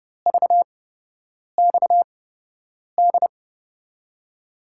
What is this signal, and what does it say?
Morse code, heard as a keyed tone, about 31 wpm, spaced as for 8 wpm, 700 Hz. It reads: VXD
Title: USA Call Signs - 5 Characters - 8x Character Spacing - 30wpm